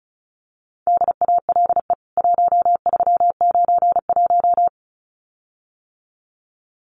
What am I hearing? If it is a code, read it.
Morse code, heard as a keyed tone, about 34 wpm, 700 Hz. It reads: DALE 1391
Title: CWT Exchange 35wpm